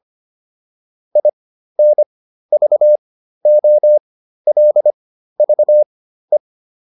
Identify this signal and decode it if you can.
Morse code, heard as a keyed tone, about 25 wpm, spaced as for 13 wpm, 600 Hz. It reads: INVOLVE